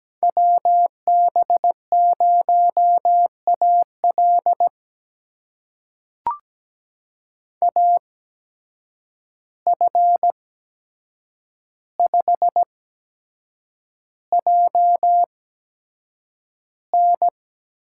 Morse code, heard as a keyed tone, about 17 wpm, spaced as for 7 wpm, 700 Hz. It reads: WB0AL AF5JN
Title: USA Call Signs - 5 Characters - 8x Character Spacing - 17wpm